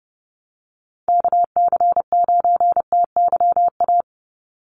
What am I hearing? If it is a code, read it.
Morse code, heard as a keyed tone, about 30 wpm, 700 Hz. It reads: KC9TYA
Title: NA SOTA Chaser Callsigns 30wpm